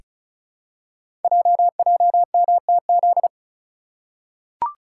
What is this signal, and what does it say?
Morse code, heard as a keyed tone, about 36 wpm, 690 Hz. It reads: JJMTZ
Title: Code Groups - 5 Letters - 35wpm